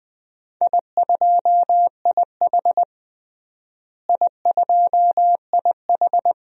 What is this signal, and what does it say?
Morse code, heard as a keyed tone, about 21 wpm, 700 Hz. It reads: I2IH I2IH
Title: DXCC-100 Call Signs - 4 Characters - Repeated 2x - 20wpm